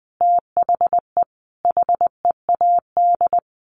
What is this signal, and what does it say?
Morse code, heard as a keyed tone, about 19 wpm, 700 Hz. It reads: THE HEAD
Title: Sets of 5 Words Encore2 20wpm